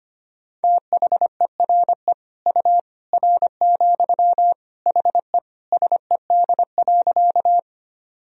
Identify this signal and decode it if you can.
Morse code, heard as a keyed tone, about 25 wpm, 700 Hz. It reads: THERE U R, HE SED.
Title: Sentences from Top 100 Words and Abbreviations 25wpm